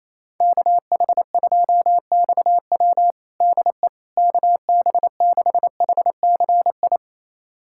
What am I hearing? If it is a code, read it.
Morse code, heard as a keyed tone, about 28 wpm, 700 Hz. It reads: KH2XW DE KB6HCI
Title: USA Callsign DE Callsign 28wpm